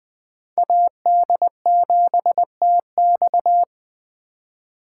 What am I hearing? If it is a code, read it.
Morse code, heard as a keyed tone, about 20 wpm, 700 Hz. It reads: AD7TX
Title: Speed Racing - USA 2x2 Call Signs 17wpm